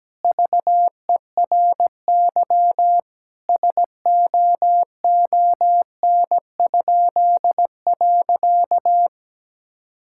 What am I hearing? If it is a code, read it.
Morse code, heard as a keyed tone, about 17 wpm, 700 Hz. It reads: VERY SOON?.
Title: Pangram Challenge v2 - 17wpm